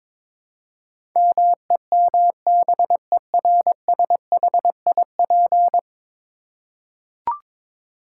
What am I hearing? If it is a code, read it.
Morse code, heard as a keyed tone, about 22 wpm, 700 Hz. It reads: MEMBERSHIP
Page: morse-code-ninja-practice-22wpm